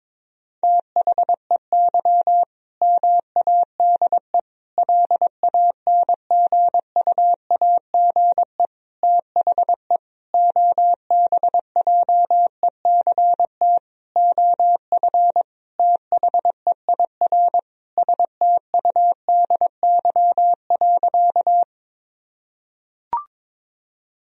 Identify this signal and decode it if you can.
Morse code, heard as a keyed tone, about 22 wpm, 700 Hz. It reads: THEY MADE LANGUAGE THE OBJECT OF THEIR STUDY.
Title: Sentences from Top 1000 Words - Review Part 3 - 22wpm